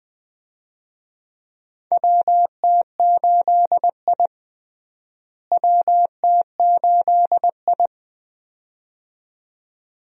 Morse code, heard as a keyed tone, about 20 wpm, 700 Hz. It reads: WT8I WT8I
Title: USA Call Signs - 4 Characters - Repeated 2x - 20wpm